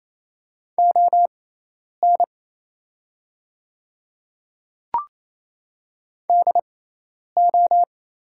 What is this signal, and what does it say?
Morse code, heard as a keyed tone, about 27 wpm, spaced as for 10 wpm, 700 Hz. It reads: ON DO
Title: 6x Character Spacing - Top 200 Words - 28wpm